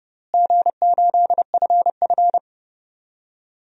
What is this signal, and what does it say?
Morse code, heard as a keyed tone, about 30 wpm, 700 Hz. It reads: G8FF
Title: DXCC-100 Call Signs - 4 Characters - Repeated 2x - 30wpm